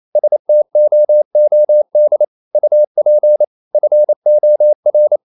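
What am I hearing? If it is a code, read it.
Morse code, heard as a keyed tone, about 28 wpm, 600 Hz. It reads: STOOD UP FOR
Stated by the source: Morse code practice at 28 wpm, text of Ken Follett's novel Never in English, from CW med Gnister.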